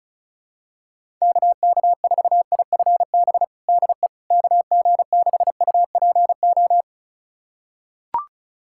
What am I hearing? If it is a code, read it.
Morse code, heard as a keyed tone, about 34 wpm, 700 Hz. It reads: KK4IFB DE KG6UPO
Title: USA Callsign DE Callsign - 6 Characters - 35wpm